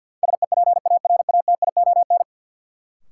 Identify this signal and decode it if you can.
Morse code, heard as a keyed tone, about 52 wpm, 700 Hz. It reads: SEPARATION
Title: Sets of 5 Words - New Word Drill - 50wpm